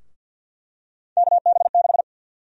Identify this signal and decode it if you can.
Morse code, heard as a keyed tone, about 52 wpm, 700 Hz. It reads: KB6
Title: USA Call Sign Prefixes - Repeated 2x - 50wpm